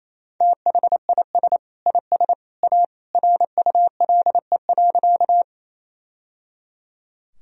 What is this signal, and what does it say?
Morse code, heard as a keyed tone, about 28 wpm, 700 Hz. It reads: THIS IS A RULE.
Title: Sentences from Top 1000 Words - Review Part 2 - 28wpm